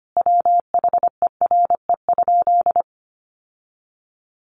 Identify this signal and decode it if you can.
Morse code, heard as a keyed tone, about 25 wpm, 700 Hz. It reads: WHERE?
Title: Sentences from Top 1200 Words 25wpm